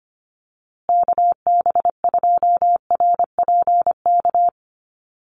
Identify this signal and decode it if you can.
Morse code, heard as a keyed tone, about 25 wpm, 700 Hz. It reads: KB2RPK
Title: USA Call Signs Repeated 2x 25wpm